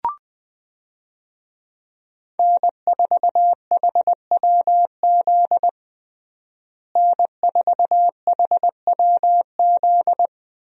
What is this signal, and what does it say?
Morse code, heard as a keyed tone, about 20 wpm, 700 Hz. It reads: N4HWZ N4HWZ
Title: USA Call Signs - 5 Characters - Repeated 2x - 20wpm